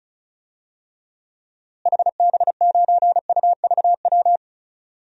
Morse code, heard as a keyed tone, about 36 wpm, 700 Hz. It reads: HB9UVW
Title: DXCC-100 Call Signs - 6 Characters - Repeated 2x - 35wpm